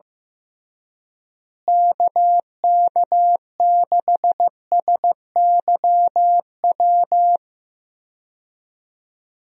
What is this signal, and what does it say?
Morse code, heard as a keyed tone, about 15 wpm, 700 Hz. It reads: KK6SYW
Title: USA Call Signs - 6 Characters - Repeated 3x - 15wpm